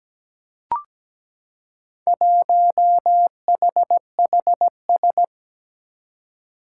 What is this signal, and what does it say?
Morse code, heard as a keyed tone, about 17 wpm, 700 Hz. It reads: 1HHS
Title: Four Challenging Letter-Number Combinations 17wpm